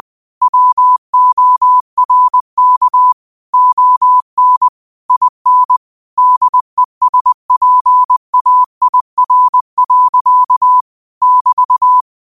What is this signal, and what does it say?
Morse code, heard as a keyed tone, about 20 wpm, 1000 Hz. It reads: WORK ON IN DESPAIR. <BT>
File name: qotdfiles / 20wpm